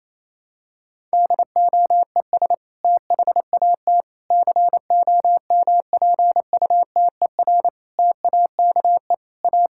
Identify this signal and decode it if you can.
Morse code, heard as a keyed tone, about 28 wpm, 700 Hz. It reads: DOES THAT COMPUTER TAKE A
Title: Sentences from Top 1000 Words - Review Part 6 - 28wpm